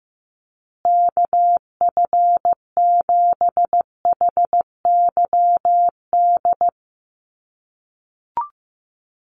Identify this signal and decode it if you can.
Morse code, heard as a keyed tone, about 15 wpm, 700 Hz. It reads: KF7HYD
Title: USA Call Signs Repeated 2x 15wpm